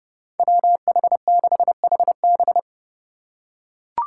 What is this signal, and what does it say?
Morse code, heard as a keyed tone, about 31 wpm, 700 Hz. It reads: WH6HB
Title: USA Call Signs - 5 Characters - 2x Character Spacing - 30wpm